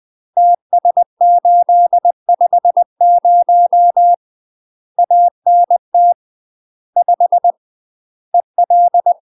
Morse code, heard as a keyed tone, about 20 wpm, 700 Hz. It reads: TS850 ANT 5 EL
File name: UK MASTER QSOs TRAINING / 20WPM / UK MASTER QSOs 0046